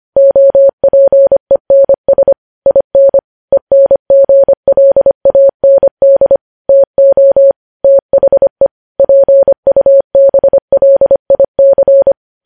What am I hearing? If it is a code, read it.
Morse code, heard as a keyed tone, about 25 wpm, 550 Hz. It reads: OPENS IN ENGLAND TO THE PUBLIC
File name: podcast / News headlines-1x-25x25